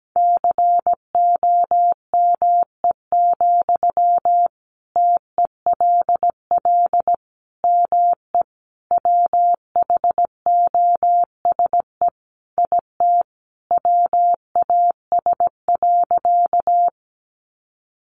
Morse code, heard as a keyed tone, about 17 wpm, 700 Hz. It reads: COME, TELL ME WHOSE IT WAS.
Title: Sentences from Top 1200 Words 17wpm